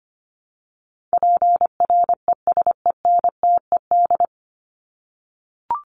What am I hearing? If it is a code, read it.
Morse code, heard as a keyed tone, about 25 wpm, 700 Hz. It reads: PRESENTED
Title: ED Suffix - Top 100 Words with Base Words 25wpm